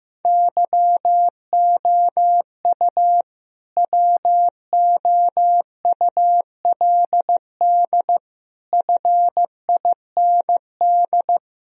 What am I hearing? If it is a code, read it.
Morse code, heard as a keyed tone, about 15 wpm, 700 Hz. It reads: YOU WOULD FIND
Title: Sentences from Top 1000 Words - Review Part 1 - 15wpm